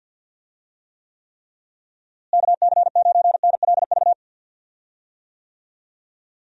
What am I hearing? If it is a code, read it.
Morse code, heard as a keyed tone, about 52 wpm, 700 Hz. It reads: KC9NLV
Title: USA Call Signs - 6 Characters - Repeated 3x - 50wpm